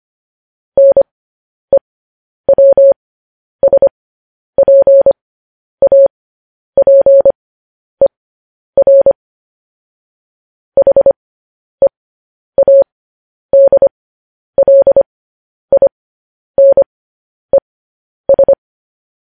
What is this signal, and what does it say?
Morse code, heard as a keyed tone, about 25 wpm, spaced as for 10 wpm, 550 Hz. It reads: NEWSPAPER HEADLINES
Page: news headlines in morse code at 10 wpm